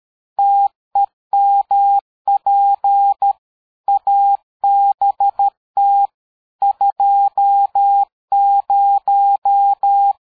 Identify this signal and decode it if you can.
Morse code, heard as a keyed tone, about 13 wpm, 800 Hz. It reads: TEMP ABT 20
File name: dk5ke / QSO 60BpM